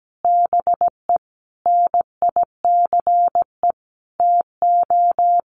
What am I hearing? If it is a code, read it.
Morse code, heard as a keyed tone, about 17 wpm, 700 Hz. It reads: BE NICE TO
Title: Sets of 5 Words Encore2 17wpm